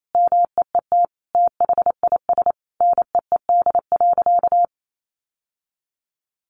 Morse code, heard as a keyed tone, about 28 wpm, 700 Hz. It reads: MEET THIS NEED.
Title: Sentences from Top 1100 Words 28wpm